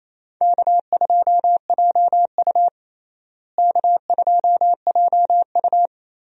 Morse code, heard as a keyed tone, about 28 wpm, 700 Hz. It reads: K2JU K2JU
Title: USA Call Signs - 4 Characters - Repeated 2x - 28wpm